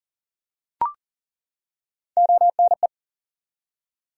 Morse code, heard as a keyed tone, about 40 wpm, 700 Hz. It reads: ONE
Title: Top 100 Words 40wpm